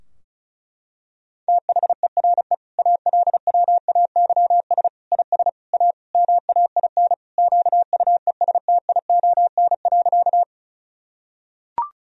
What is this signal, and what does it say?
Morse code, heard as a keyed tone, about 34 wpm, 700 Hz. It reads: THERE ALWAYS IS A MAIN QUESTION.
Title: Sentences from Top 1000 Words - Review Part 2 - 35wpm